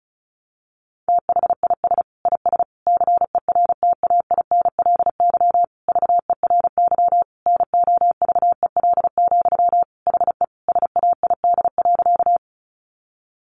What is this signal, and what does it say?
Morse code, heard as a keyed tone, about 34 wpm, 700 Hz. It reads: THIS IS CERTAINLY VERY NOVEL, HE SAID.
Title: Sentences from Top 1300 Words 35wpm